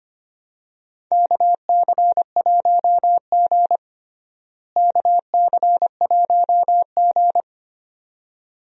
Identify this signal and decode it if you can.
Morse code, heard as a keyed tone, about 25 wpm, 700 Hz. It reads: KC1G KC1G
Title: USA Call Signs - 4 Characters - Repeated 2x - 25wpm